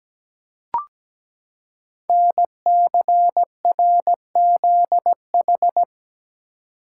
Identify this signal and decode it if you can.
Morse code, heard as a keyed tone, about 17 wpm, 700 Hz. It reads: NCRZH
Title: Code Groups - 5 Letters - 17wpm